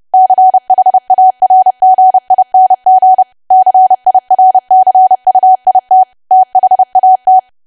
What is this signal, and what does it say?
Morse code, heard as a keyed tone, about 30 wpm, 750 Hz. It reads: CHARGING CIRCUIT THAT
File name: cwa / W1AW-Level3 / 30WPM